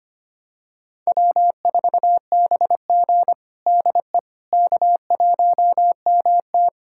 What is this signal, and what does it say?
Morse code, heard as a keyed tone, about 25 wpm, 700 Hz. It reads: W4BG DE K1MT
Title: USA Callsign DE Callsign - 4 Characters - 25wpm